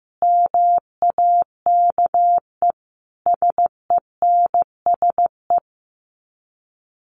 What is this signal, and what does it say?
Morse code, heard as a keyed tone, about 15 wpm, 700 Hz. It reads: MAKE SENSE
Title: Sets of 5 Words Encore2 15wpm